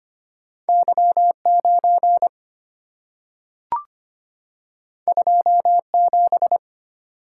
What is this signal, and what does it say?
Morse code, heard as a keyed tone, about 25 wpm, 700 Hz. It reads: Y9 27
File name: Two Letter-Number Combinations - 25wpm